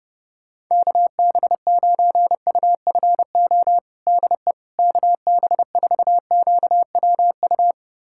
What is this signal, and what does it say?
Morse code, heard as a keyed tone, about 30 wpm, 700 Hz. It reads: KB9UFO DE KB4QWU
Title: USA Callsign DE Callsign - 6 Characters - 30wpm